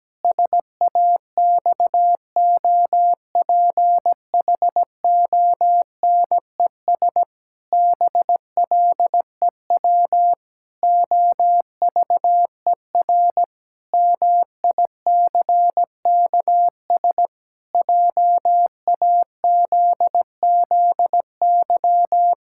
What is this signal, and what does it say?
Morse code, heard as a keyed tone, about 17 wpm, 700 Hz. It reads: SAXOPHONES BLEW OVER MICKS JAZZY
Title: Pangram Challenge v2 - 17wpm